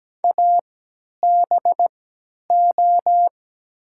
Morse code, heard as a keyed tone, about 17 wpm, spaced as for 10 wpm, 700 Hz. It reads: ABO
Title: 3x Character Spacing - Top 500 Words - 17wpm